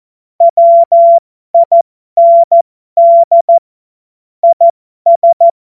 Morse code, heard as a keyed tone, about 14 wpm, spaced as for 12 wpm, 680 Hz. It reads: WIND IS